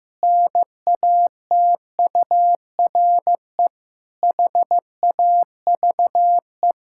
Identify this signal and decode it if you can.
Morse code, heard as a keyed tone, about 15 wpm, 700 Hz. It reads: NATURE HAVE
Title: Pangram Challenge v2 - 15wpm